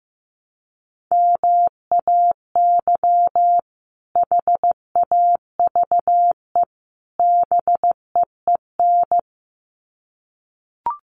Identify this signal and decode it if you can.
Morse code, heard as a keyed tone, about 15 wpm, 700 Hz. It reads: MAY HAVE BEEN
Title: Sets of 3 Words 15wpm